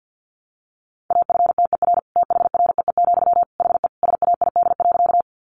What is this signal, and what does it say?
Morse code, heard as a keyed tone, about 50 wpm, 700 Hz. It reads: AFTER THREE, HE SAID.
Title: Sentences from Top 200 Words 50wpm